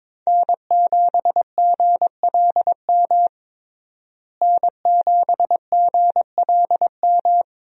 Morse code, heard as a keyed tone, about 22 wpm, 700 Hz. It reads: N7GLM N7GLM
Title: USA Call Signs - 5 Characters - Repeated 2x - 22wpm